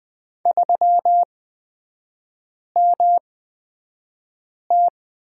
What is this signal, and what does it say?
Morse code, heard as a keyed tone, about 20 wpm, spaced as for 5 wpm, 700 Hz. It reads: 3MT
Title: 243 Lesson 39e - Letter z - Three Characters 20wpm